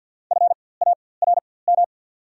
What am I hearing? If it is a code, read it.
Morse code, heard as a keyed tone, about 54 wpm, spaced as for 24 wpm, 690 Hz. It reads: FARM
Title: 4x Character Spacing - Top 500 Words - 50wpm